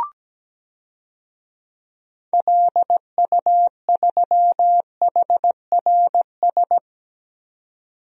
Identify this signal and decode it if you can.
Morse code, heard as a keyed tone, about 17 wpm, 700 Hz. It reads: LU3HRS
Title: DXCC-100 Call Signs - 6 Characters - 17wpm